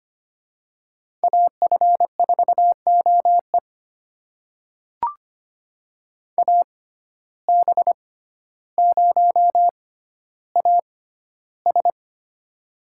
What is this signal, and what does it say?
Morse code, heard as a keyed tone, about 25 wpm, spaced as for 13 wpm, 700 Hz. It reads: AF4OE AB0AS
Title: USA Call Signs - 5 Characters - 6x Character Spacing - 25wpm